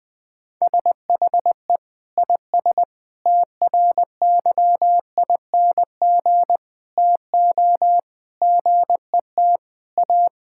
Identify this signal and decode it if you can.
Morse code, heard as a keyed tone, about 20 wpm, 700 Hz. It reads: SHE IS TRYING TO GET A
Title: Sentences with 3-Word Phrases - Spoken Beginning - 20wpm